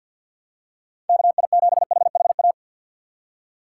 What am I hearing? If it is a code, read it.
Morse code, heard as a keyed tone, about 54 wpm, 700 Hz. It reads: KI7HHA
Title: USA Call Signs - 6 Characters - 3x Character Spacing - 50wpm